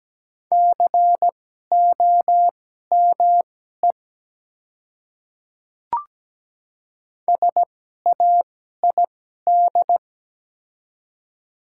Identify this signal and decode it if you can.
Morse code, heard as a keyed tone, about 17 wpm, spaced as for 12 wpm, 700 Hz. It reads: COME SAID
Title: 2x Character Spacing - Top 200 Words - 17wpm